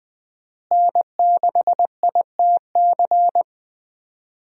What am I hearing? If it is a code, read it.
Morse code, heard as a keyed tone, about 20 wpm, 700 Hz. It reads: N6ITC
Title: USA Call Signs - 5 Characters - Repeated 2x - 20wpm